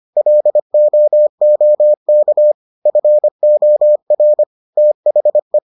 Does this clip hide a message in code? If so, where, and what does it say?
Yes: Morse code from 0.2 to 5.6 s, heard as a keyed tone, about 25 wpm, 600 Hz. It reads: LOOK FOR THE